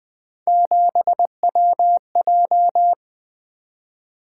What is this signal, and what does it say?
Morse code, heard as a keyed tone, about 20 wpm, 700 Hz. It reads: 7WJ
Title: Three Letter-Number Combinations - 20wpm